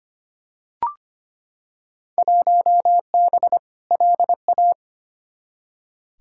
Morse code, heard as a keyed tone, about 25 wpm, 700 Hz. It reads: 1B LA
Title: Field Day Exchanges - 25wpm